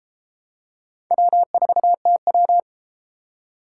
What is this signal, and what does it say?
Morse code, heard as a keyed tone, about 32 wpm, 700 Hz. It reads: W4TW